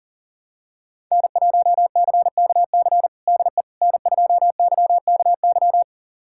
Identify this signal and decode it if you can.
Morse code, heard as a keyed tone, about 40 wpm, 700 Hz. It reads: N1CKC DE N2YKY
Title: USA Callsign DE Callsign - 5 Characters - 40wpm